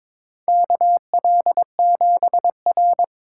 Morse code, heard as a keyed tone, about 22 wpm, 700 Hz. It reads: KL7R
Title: USA Call Signs - 4x Character Spacing - 22wpm